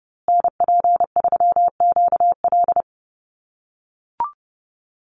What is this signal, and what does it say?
Morse code, heard as a keyed tone, about 30 wpm, 700 Hz. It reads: NP3QL
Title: USA Call Signs Repeated 3x 30wpm